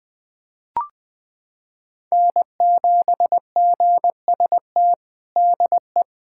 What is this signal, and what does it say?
Morse code, heard as a keyed tone, about 20 wpm, 700 Hz. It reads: N7GST DE
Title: USA Callsign DE Callsign - 5 Characters - 20wpm